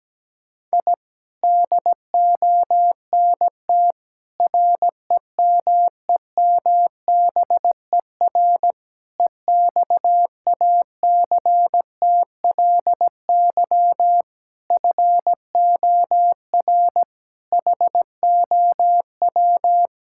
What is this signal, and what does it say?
Morse code, heard as a keyed tone, about 17 wpm, 700 Hz. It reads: I DONT REMEMBER EXACTLY FOR HOW
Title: Sentences from Top 1000 Words - Review Part 6 - 17wpm